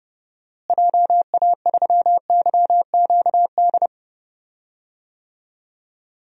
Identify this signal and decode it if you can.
Morse code, heard as a keyed tone, about 30 wpm, 700 Hz. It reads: JA3YQD
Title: DXCC-100 Call Signs - 6 Characters - Repeated 3x - 30wpm